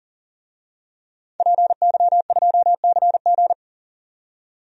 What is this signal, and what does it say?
Morse code, heard as a keyed tone, about 40 wpm, 700 Hz. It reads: PY2CG
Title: DXCC-100 Call Signs - 5 Characters - 40wpm